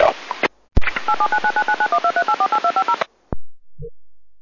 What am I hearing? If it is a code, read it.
DTMF, heard as telephone touch tones, tones of about 80 ms. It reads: 8799999123848250